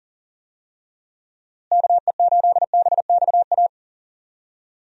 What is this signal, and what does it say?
Morse code, heard as a keyed tone, about 40 wpm, 700 Hz. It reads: KE8BXA
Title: USA Call Signs - 6 Characters - Repeated 3x - 40wpm